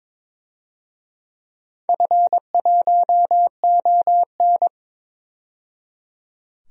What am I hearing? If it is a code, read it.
Morse code, heard as a keyed tone, about 21 wpm, 700 Hz. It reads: F1ON
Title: DXCC-100 Call Signs - 4 Characters - 22wpm